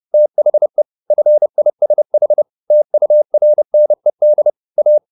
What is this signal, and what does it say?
Morse code, heard as a keyed tone, about 30 wpm, 600 Hz. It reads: THE FISH TURNED A